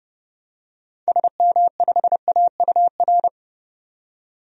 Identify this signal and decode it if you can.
Morse code, heard as a keyed tone, about 30 wpm, 700 Hz. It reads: SM5AUR
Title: DXCC-100 Call Signs - 6 Characters - 30wpm